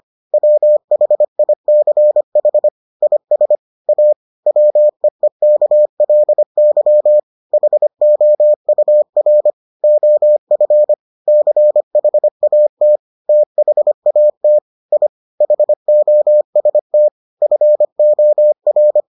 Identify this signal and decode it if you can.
Morse code, heard as a keyed tone, about 25 wpm, 600 Hz. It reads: WHICH IS A WEEKLY HOUR OF CHAT THAT I HOST FOR